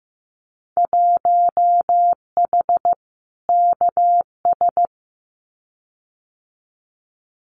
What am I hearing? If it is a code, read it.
Morse code, heard as a keyed tone, about 15 wpm, 700 Hz. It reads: 1H KS
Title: Winter Field Day Exchanges 15wpm